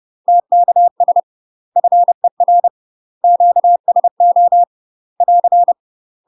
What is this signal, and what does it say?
Morse code, heard as a keyed tone, about 30 wpm, 700 Hz. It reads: TKS FER QSO <AR>
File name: UK MASTER QSOs TRAINING / 30WPM / UK MASTER QSOs 0045